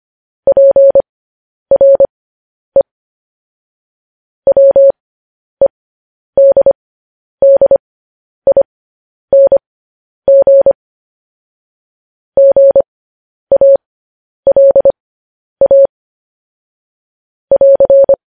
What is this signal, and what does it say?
Morse code, heard as a keyed tone, about 25 wpm, spaced as for 10 wpm, 550 Hz. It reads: PRE WEDDING GALA <AR>
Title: News Headlines at 10